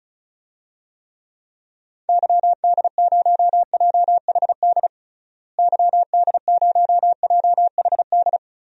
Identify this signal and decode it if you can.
Morse code, heard as a keyed tone, about 36 wpm, 700 Hz. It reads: YD0JHD YD0JHD
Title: DXCC-100 Call Signs - 6 Characters - Repeated 2x - 35wpm